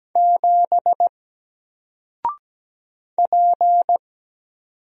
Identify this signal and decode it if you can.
Morse code, heard as a keyed tone, about 17 wpm, 700 Hz. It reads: 7 P
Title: Single Character (Letter:Number:Punctuation:Prosign) 17wpm